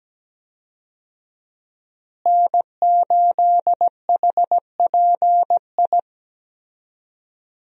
Morse code, heard as a keyed tone, about 17 wpm, 700 Hz. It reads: N8HPI
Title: USA Call Signs - 5 Characters - Repeated 2x - 17wpm